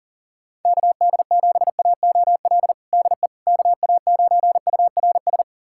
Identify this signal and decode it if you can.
Morse code, heard as a keyed tone, about 40 wpm, 700 Hz. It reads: KD7AOL DE KA9URS
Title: USA Callsign DE Callsign 40wpm